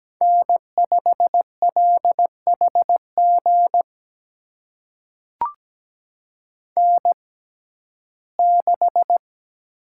Morse code, heard as a keyed tone, about 17 wpm, spaced as for 12 wpm, 700 Hz. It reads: N5LHG N6
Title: USA Call Signs - 5 Characters - 6x Character Spacing - 17wpm